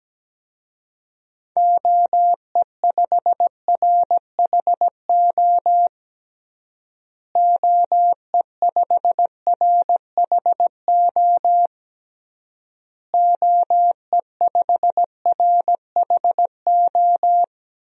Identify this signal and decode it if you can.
Morse code, heard as a keyed tone, about 17 wpm, 700 Hz. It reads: OE5RHO OE5RHO OE5RHO
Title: DXCC-100 Call Signs - 6 Characters - Repeated 3x - 17wpm